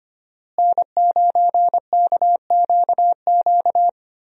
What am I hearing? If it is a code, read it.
Morse code, heard as a keyed tone, about 25 wpm, 700 Hz. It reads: N9KQQ
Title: USA Call Signs - 5 Characters - Repeated 2x - 25wpm